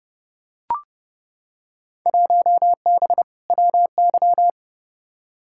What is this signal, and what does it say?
Morse code, heard as a keyed tone, about 29 wpm, 700 Hz. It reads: 1B WY
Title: Field Day Exchanges Encore - 30wpm